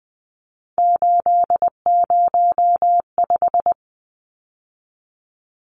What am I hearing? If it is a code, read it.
Morse code, heard as a keyed tone, about 20 wpm, 700 Hz. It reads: 805
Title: Numbers - 3-Digit 20wpm